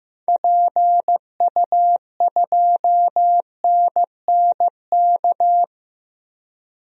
Morse code, heard as a keyed tone, about 15 wpm, 700 Hz. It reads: PU2NNK
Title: DXCC-100 Call Signs - 6 Characters - Repeated 2x - 15wpm